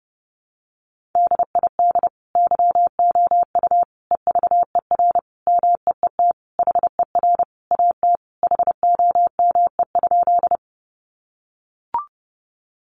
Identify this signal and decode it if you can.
Morse code, heard as a keyed tone, about 30 wpm, 700 Hz. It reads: DID YOU EVER MEET HER AT HOME?
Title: Sentences from Top 1100 Words 30wpm